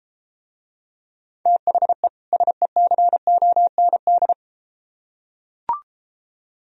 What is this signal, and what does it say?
Morse code, heard as a keyed tone, about 33 wpm, 700 Hz. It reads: THE SECOND